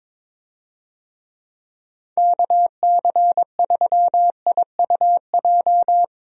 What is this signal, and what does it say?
Morse code, heard as a keyed tone, about 22 wpm, 700 Hz. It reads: KC3IUJ
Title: USA Call Signs - 6 Characters - 22wpm